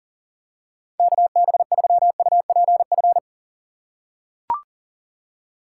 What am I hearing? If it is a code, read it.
Morse code, heard as a keyed tone, about 40 wpm, 700 Hz. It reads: KB3UPF
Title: USA Call Signs - 6 Characters - 2x Character Spacing - 40wpm